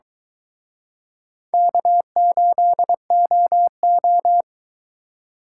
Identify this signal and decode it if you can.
Morse code, heard as a keyed tone, about 21 wpm, 700 Hz. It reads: K8OO
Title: Speed Racing - USA Call Signs 17wpm